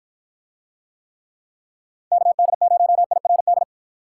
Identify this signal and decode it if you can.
Morse code, heard as a keyed tone, about 53 wpm, 700 Hz. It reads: KD0IRD